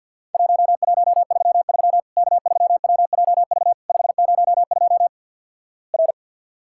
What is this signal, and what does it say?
Morse code, heard as a keyed tone, about 47 wpm, 700 Hz. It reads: 1133 K3WJV 502 R
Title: Worked All Europe DX Contest - QTC Exchange - 50wpm